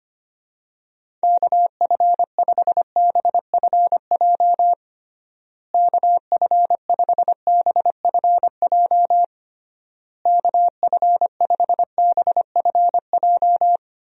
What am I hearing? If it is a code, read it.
Morse code, heard as a keyed tone, about 25 wpm, 700 Hz. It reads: KF5BFJ KF5BFJ KF5BFJ
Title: USA Call Signs - 6 Characters - Repeated 3x - 25wpm